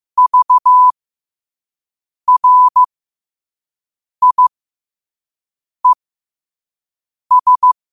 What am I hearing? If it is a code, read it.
Morse code, heard as a keyed tone, about 15 wpm, spaced as for 5 wpm, 1000 Hz. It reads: VRIES
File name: qotdfiles / 5wpm